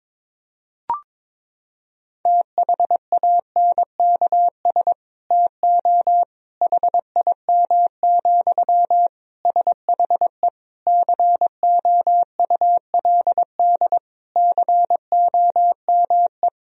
Morse code, heard as a keyed tone, about 22 wpm, 700 Hz. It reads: THANKS TO HIM, SHE COULD COME
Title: Sentences from Top 1000 Words - Review Part 3 - 22wpm